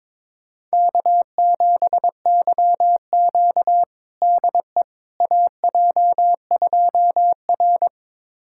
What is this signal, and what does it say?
Morse code, heard as a keyed tone, about 22 wpm, 700 Hz. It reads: K7YQ DE AJ2R
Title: USA Callsign DE Callsign - 4 Characters - 22wpm